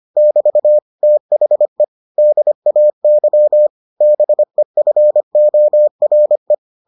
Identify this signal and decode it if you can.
Morse code, heard as a keyed tone, about 25 wpm, 600 Hz. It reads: <BT> THE DAY BEFORE